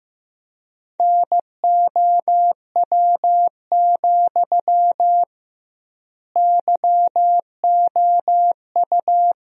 Morse code, heard as a keyed tone, about 15 wpm, 700 Hz. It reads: NOW, YOU
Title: 2x Word-Spacing - Sentences from Top 200 Words - 15wpm